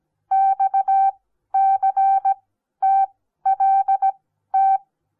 Morse code, heard as a keyed tone, about 17 wpm, spaced as for 12 wpm, 780 Hz. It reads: XCTLT
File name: Onlinelehrgang / morsen / DJ4UF Morsen08